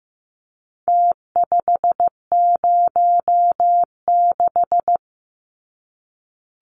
Morse code, heard as a keyed tone, about 15 wpm, 700 Hz. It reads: T506
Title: Numbers - Contest Style - 4-Digit 15wpm